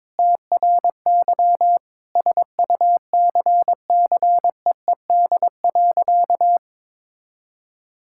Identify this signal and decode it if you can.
Morse code, heard as a keyed tone, about 21 wpm, 700 Hz. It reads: TRY SUCCEED.
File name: practice / 22wpm / Sentences with 2-Word Phrases - 22wpm